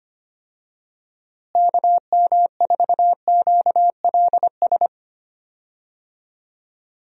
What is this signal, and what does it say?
Morse code, heard as a keyed tone, about 25 wpm, 700 Hz. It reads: KM4QLS
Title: USA Call Signs - 6 Characters - Repeated 2x - 25wpm